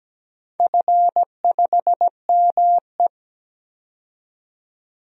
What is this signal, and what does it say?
Morse code, heard as a keyed tone, about 17 wpm, 700 Hz. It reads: F5ME
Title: DXCC-100 Call Signs - 4 Characters - 17wpm